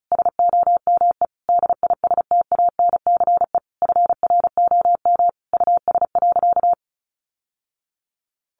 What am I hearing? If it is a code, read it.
Morse code, heard as a keyed tone, about 34 wpm, 700 Hz. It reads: SOME DISTANCE FROM US.
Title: Sentences from Top 1200 Words 35wpm